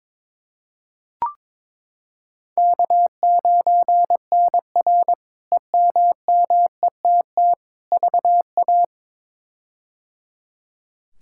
Morse code, heard as a keyed tone, about 22 wpm, 700 Hz. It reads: K9NR EMMETT VA
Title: K1USN SST QSO Practice - 22wpm